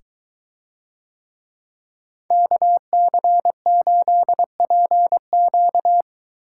Morse code, heard as a keyed tone, about 22 wpm, 700 Hz. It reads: KC8PQ